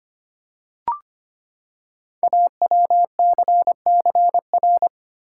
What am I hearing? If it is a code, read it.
Morse code, heard as a keyed tone, about 25 wpm, 700 Hz. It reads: AWCCR
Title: Code Groups - 5 Letters - 25wpm